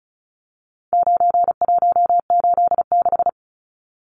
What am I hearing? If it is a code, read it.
Morse code, heard as a keyed tone, about 34 wpm, 700 Hz. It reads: 9186
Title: Numbers - 4-Digit 35wpm